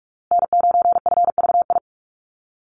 Morse code, heard as a keyed tone, about 44 wpm, 700 Hz. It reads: N9FVI
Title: USA Call Signs Repeated 3x 45wpm